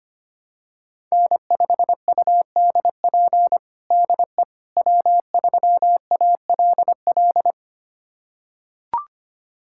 Morse code, heard as a keyed tone, about 25 wpm, 700 Hz. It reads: N5UDP DE W3ALL
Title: USA Callsign DE Callsign - 5 Characters - 25wpm